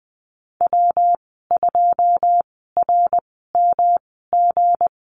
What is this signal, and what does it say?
Morse code, heard as a keyed tone, about 20 wpm, spaced as for 15 wpm, 700 Hz. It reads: W2RMG
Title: 173 Lesson 28g - Number 9 - Call Signs 20wpm